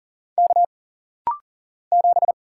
Morse code, heard as a keyed tone, about 39 wpm, 700 Hz. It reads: K 7
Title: Single Letter-Number Alt - Rapid-Fire 40wpm